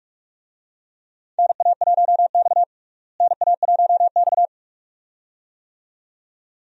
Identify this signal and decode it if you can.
Morse code, heard as a keyed tone, about 46 wpm, 700 Hz. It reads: NA1X NA1X
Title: USA Call Signs - 4 Characters - Repeated 2x - 45wpm